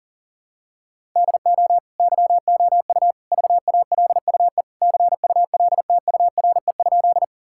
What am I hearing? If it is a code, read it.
Morse code, heard as a keyed tone, about 37 wpm, 700 Hz. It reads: DO YOU VALUE CULTURE?
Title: Sentences from Top 1000 Words - Review Part 6 - 40wpm